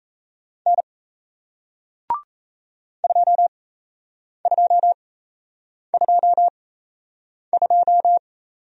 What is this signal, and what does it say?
Morse code, heard as a keyed tone, about 38 wpm, spaced as for 8 wpm, 700 Hz. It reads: N 2222